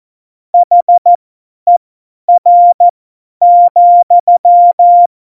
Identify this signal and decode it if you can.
Morse code, heard as a keyed tone, about 14 wpm, spaced as for 10 wpm, 700 Hz. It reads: HER,
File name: Morsekurs / Witze / Witz 20. Oktober 2020, 50 F70